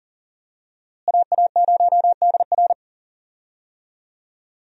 Morse code, heard as a keyed tone, about 40 wpm, 700 Hz. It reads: AA0DR
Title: USA Call Signs - 5 Characters - Repeated 2x - 40wpm